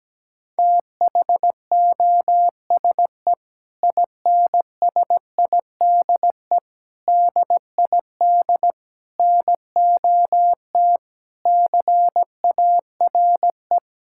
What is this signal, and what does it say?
Morse code, heard as a keyed tone, about 17 wpm, 700 Hz. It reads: THOSE INSIDE DID NOT CARE
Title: Sentences from Top 1000 Words - Review Part 6 - 17wpm